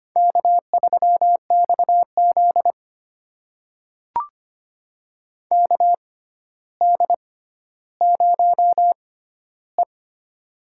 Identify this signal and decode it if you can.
Morse code, heard as a keyed tone, about 25 wpm, spaced as for 13 wpm, 700 Hz. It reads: K3XZ KD0E
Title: USA Call Signs - 4 Characters - 6x Character Spacing - 25wpm